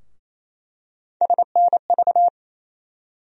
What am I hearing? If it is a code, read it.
Morse code, heard as a keyed tone, about 28 wpm, 700 Hz. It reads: SNV
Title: USA Call Sign Postfixes - 3 Letters 28wpm